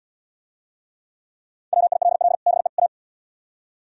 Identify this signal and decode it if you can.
Morse code, heard as a keyed tone, about 76 wpm, 700 Hz. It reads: WILL BE A